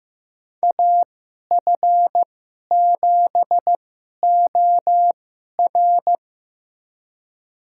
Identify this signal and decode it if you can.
Morse code, heard as a keyed tone, about 15 wpm, spaced as for 11 wpm, 700 Hz. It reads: AF7OR
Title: USA Call Signs - 5 Characters - 2x Character Spacing - 15wpm